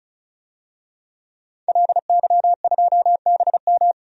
Morse code, heard as a keyed tone, about 36 wpm, 700 Hz. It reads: LY2BM
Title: DXCC-100 Call Signs - 5 Characters - 35wpm